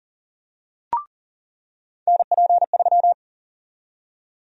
Morse code, heard as a keyed tone, about 40 wpm, 700 Hz. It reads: NP3
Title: USA Call Sign Prefixes - 40wpm